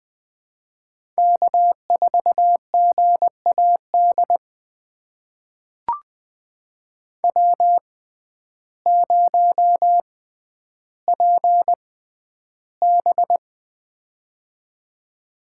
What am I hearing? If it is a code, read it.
Morse code, heard as a keyed tone, about 20 wpm, spaced as for 11 wpm, 700 Hz. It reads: K4GAD W0PB
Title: USA Call Signs - 6x Character Spacing - 20wpm